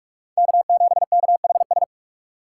Morse code, heard as a keyed tone, about 43 wpm, 700 Hz. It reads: K7KHS
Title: USA Call Signs - 5 Characters - 2x Character Spacing - 45wpm